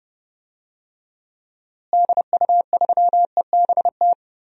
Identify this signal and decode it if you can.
Morse code, heard as a keyed tone, about 32 wpm, 700 Hz. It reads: DU3EBT
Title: DXCC-100 Call Signs - 6 Characters - Repeated 2x - 30wpm